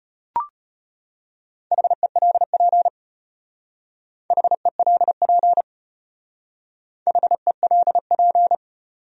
Morse code, heard as a keyed tone, about 31 wpm, 700 Hz. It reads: HELP HELP HELP
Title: Speed-Racing Top 200 Words 25wpm